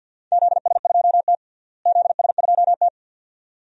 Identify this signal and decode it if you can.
Morse code, heard as a keyed tone, about 52 wpm, 700 Hz. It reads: ZS2T ZS2T
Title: DXCC-100 Call Signs - 4 Characters - Repeated 2x - 50wpm